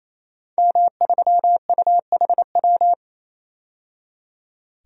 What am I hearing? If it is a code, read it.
Morse code, heard as a keyed tone, about 28 wpm, 700 Hz. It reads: M3UHW
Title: DXCC-100 Call Signs - 5 Characters - 28wpm